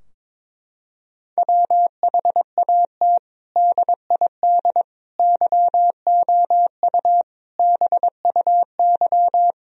Morse code, heard as a keyed tone, about 22 wpm, 700 Hz. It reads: WHAT DID YOU BUY
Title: Sentences from Top 1000 Words - Review Part 4 - 22wpm